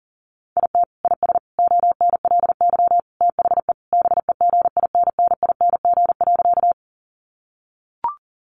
Morse code, heard as a keyed tone, about 40 wpm, 700 Hz. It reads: IT IS ONLY THE BEGINNING.
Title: Sentences from Top 800 Words 40wpm